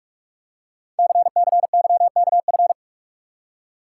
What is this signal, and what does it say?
Morse code, heard as a keyed tone, about 44 wpm, 700 Hz. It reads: CCYKF
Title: Code Groups - 5 Letters - 45wpm